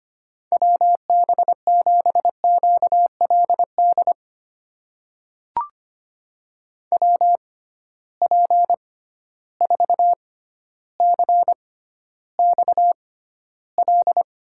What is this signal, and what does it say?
Morse code, heard as a keyed tone, about 25 wpm, spaced as for 13 wpm, 700 Hz. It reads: WB7QLD WP4CXL
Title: USA Call Signs - 6 Characters - 6x Character Spacing - 25wpm